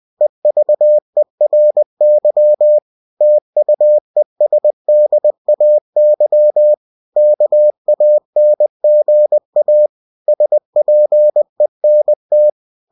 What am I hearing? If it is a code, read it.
Morse code, heard as a keyed tone, about 20 wpm, 600 Hz. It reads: EVERY TUESDAY KANGA SPENT
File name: cw-books / winnie-the-pooh / winnie-20wpm-page-0111